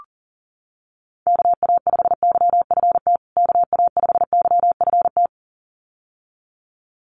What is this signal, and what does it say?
Morse code, heard as a keyed tone, about 37 wpm, 700 Hz. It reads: KA5YFT KA5YFT
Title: USA Call Signs Repeated 2x 40wpm